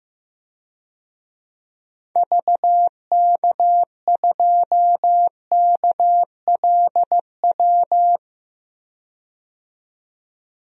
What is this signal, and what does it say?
Morse code, heard as a keyed tone, about 15 wpm, 700 Hz. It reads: VK2KLW
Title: DXCC-100 Call Signs - 6 Characters - 15wpm